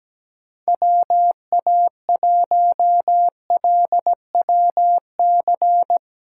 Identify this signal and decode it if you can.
Morse code, heard as a keyed tone, about 17 wpm, 700 Hz. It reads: WA1LWC
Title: K1USN SST QSO Practice - 17wpm